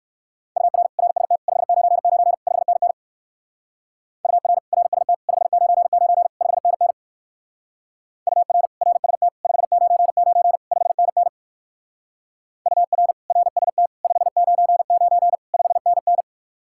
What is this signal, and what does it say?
Morse code, heard as a keyed tone, about 55 wpm, 700 Hz. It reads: UR RST 599 5NN UR RST 599 5NN UR RST 599 5NN UR RST 599 5NN
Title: Speed-Racing Common QSO Phrases - 45wpm